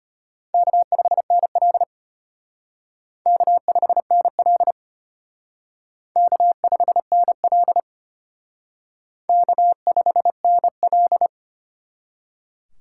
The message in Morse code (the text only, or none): K5NL K5NL K5NL K5NL